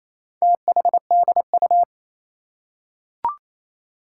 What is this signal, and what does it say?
Morse code, heard as a keyed tone, about 27 wpm, 700 Hz. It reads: THDU
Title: Four Challenging Letter-Number Combinations 28wpm